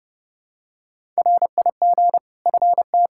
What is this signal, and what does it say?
Morse code, heard as a keyed tone, about 30 wpm, 700 Hz. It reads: RIG FT
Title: Practice QSO - Exchange 2 - Info - 30wpm